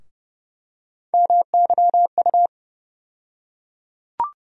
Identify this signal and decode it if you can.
Morse code, heard as a keyed tone, about 30 wpm, 700 Hz. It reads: MYU
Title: USA Call Sign Postfixes - 3 Letters 30wpm